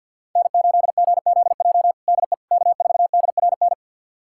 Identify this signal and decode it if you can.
Morse code, heard as a keyed tone, about 52 wpm, 700 Hz. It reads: N8GZJ DE K4DRN
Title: USA Callsign DE Callsign 50wpm